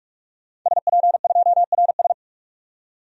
Morse code, heard as a keyed tone, about 46 wpm, 700 Hz. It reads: SP2RS
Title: DXCC-100 Call Signs - 5 Characters - 45wpm